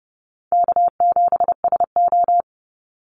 Morse code, heard as a keyed tone, about 30 wpm, 700 Hz. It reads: K7SO
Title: NA SOTA Activator Callsigns 30wpm